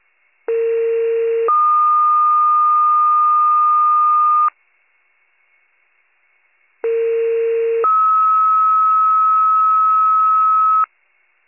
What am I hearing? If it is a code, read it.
Two-tone page, heard as a one-second tone, then a three-second tone, about 460 then 1220 Hz, about 460 then 1290 Hz. It is heard from 0.5 to 10.8 s.